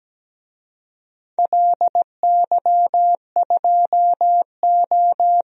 Morse code, heard as a keyed tone, about 17 wpm, 700 Hz. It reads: LY2O
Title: DXCC-100 Call Signs - 4 Characters - 17wpm